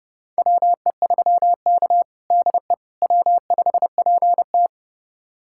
Morse code, heard as a keyed tone, about 32 wpm, 700 Hz. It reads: WE3K DE W5PT
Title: USA Callsign DE Callsign - 4 Characters - 30wpm